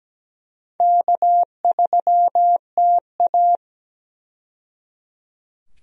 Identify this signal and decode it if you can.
Morse code, heard as a keyed tone, about 17 wpm, 700 Hz. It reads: K3TA
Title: USA Call Signs - 4 Characters - 17wpm